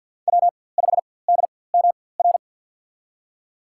Morse code, heard as a keyed tone, about 51 wpm, spaced as for 23 wpm, 700 Hz. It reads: W5DMR
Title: USA Call Signs - 4x Character Spacing - 50wpm